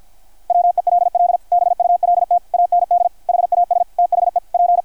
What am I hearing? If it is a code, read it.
Morse code, heard as a keyed tone, about 53 wpm, 690 Hz. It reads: WELL BULT AND HAS THE P
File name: archived / CRSnet